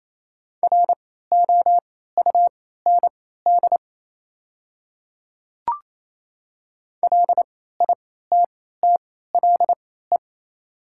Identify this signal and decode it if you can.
Morse code, heard as a keyed tone, about 27 wpm, spaced as for 16 wpm, 700 Hz. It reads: ROUND LITTLE
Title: 3x Character Spacing - Top 300 Words - 28wpm